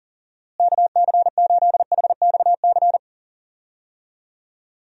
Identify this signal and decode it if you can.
Morse code, heard as a keyed tone, about 37 wpm, 700 Hz. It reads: KC8HXC
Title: USA Call Signs - 6 Characters - Repeated 2x - 40wpm